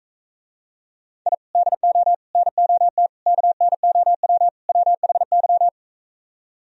Morse code, heard as a keyed tone, about 45 wpm, 700 Hz. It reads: I DO NOT KNOW WHY